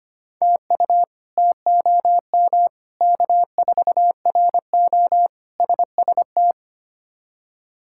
Morse code, heard as a keyed tone, about 25 wpm, 700 Hz. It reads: TU TOM K4RO SST
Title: K1USN SST QSO Practice - 25wpm